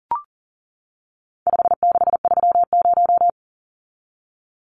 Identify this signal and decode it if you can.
Morse code, heard as a keyed tone, about 40 wpm, 700 Hz. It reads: 5630
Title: Numbers - 4-Digit 40wpm